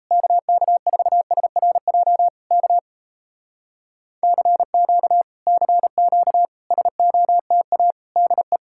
Morse code, heard as a keyed tone, about 37 wpm, 700 Hz. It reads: KK4SRJ K CQ CQ SOTA DE